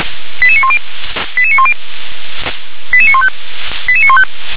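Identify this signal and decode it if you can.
Five-tone selective call (ZVEI), heard as quick runs of five tones, tones of about 70 ms. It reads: 88910 88910 88915 88915
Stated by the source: ZVEI five-tone call 88150 received by SDR.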